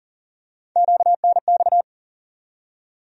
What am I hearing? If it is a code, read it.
Morse code, heard as a keyed tone, about 40 wpm, 700 Hz. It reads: QNX
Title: USA Call Sign Postfixes - 3 Letters 40wpm